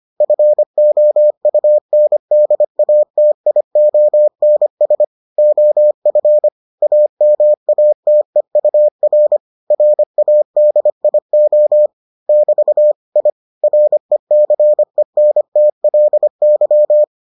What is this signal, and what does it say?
Morse code, heard as a keyed tone, about 25 wpm, 600 Hz. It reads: FOUNDATIONS OF AMATEUR RADIO <BT> I RECENTLY